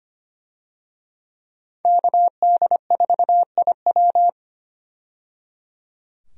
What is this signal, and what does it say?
Morse code, heard as a keyed tone, about 25 wpm, 700 Hz. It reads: KD4IW
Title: USA Call Signs - 5 Characters - 25wpm